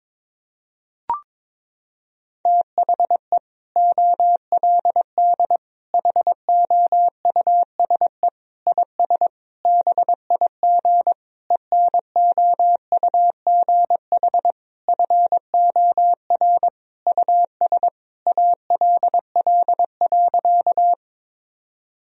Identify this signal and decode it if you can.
Morse code, heard as a keyed tone, about 22 wpm, 700 Hz. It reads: THE OLD HOUSE IS BIG ENOUGH FOR US ALL.
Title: Sentences from Top 1000 Words - Review Part 1 - 22wpm